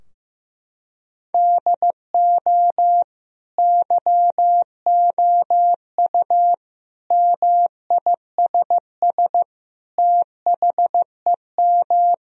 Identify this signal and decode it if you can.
Morse code, heard as a keyed tone, about 14 wpm, 700 Hz. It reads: DO YOU MISS THEM
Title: Sentences from Top 1000 Words - Review Part 3 - 15wpm